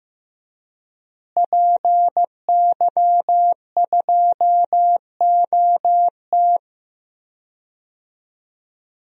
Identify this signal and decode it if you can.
Morse code, heard as a keyed tone, about 15 wpm, 700 Hz. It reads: PY2OT
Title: DXCC-100 Call Signs - 5 Characters - Repeated 3x - 15wpm